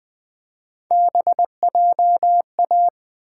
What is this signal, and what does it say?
Morse code, heard as a keyed tone, about 20 wpm, 700 Hz. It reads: BJA
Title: Three Letter-Number Combinations - 20wpm